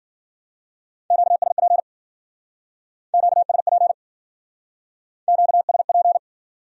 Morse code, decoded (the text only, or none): QSP QSP QSP